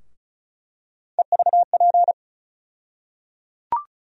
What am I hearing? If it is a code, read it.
Morse code, heard as a keyed tone, about 36 wpm, 700 Hz. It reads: EVP
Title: USA Call Sign Postfixes - 3 Letters 35wpm